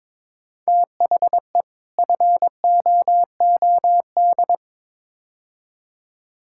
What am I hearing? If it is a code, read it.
Morse code, heard as a keyed tone, about 22 wpm, 700 Hz. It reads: THE FOOD
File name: practice / 22wpm / Build A Phrase - 5-Word Phrases - 22wpm